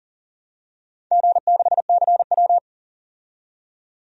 Morse code, heard as a keyed tone, about 38 wpm, 700 Hz. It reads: G6CW
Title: DXCC-100 Call Signs - 4 Characters - 40wpm